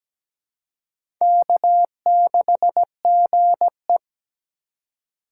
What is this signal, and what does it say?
Morse code, heard as a keyed tone, about 17 wpm, 700 Hz. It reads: K6GE
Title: USA Call Signs - 2x Character Spacing - 17wpm